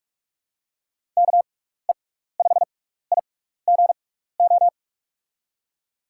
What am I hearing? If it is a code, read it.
Morse code, heard as a keyed tone, about 44 wpm, spaced as for 15 wpm, 700 Hz. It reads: KE5IGO
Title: USA Call Signs - 6 Characters - 6x Character Spacing - 45wpm